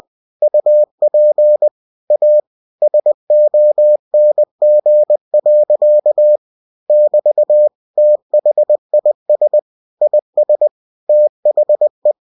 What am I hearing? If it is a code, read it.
Morse code, heard as a keyed tone, about 20 wpm, 600 Hz. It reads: UP A SONG. <BT> THIS IS THE
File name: winnie-20wpm